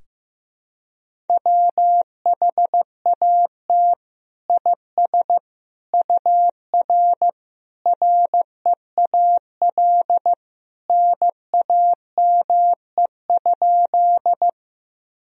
Morse code, heard as a keyed tone, about 15 wpm, 700 Hz. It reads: WHAT IS UR REAL NAME?
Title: Sentences from Top 300 Words and Abbreviations 15wpm